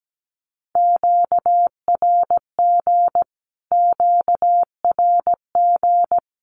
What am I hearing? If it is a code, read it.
Morse code, heard as a keyed tone, about 17 wpm, 700 Hz. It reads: QRG QRG
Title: Q-Codes Repeated 3x 17wpm